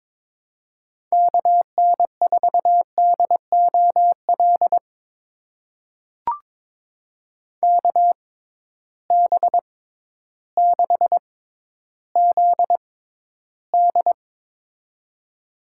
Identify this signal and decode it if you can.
Morse code, heard as a keyed tone, about 23 wpm, spaced as for 12 wpm, 700 Hz. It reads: KN4DOL KB6ZD
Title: USA Call Signs - 6x Character Spacing - 22wpm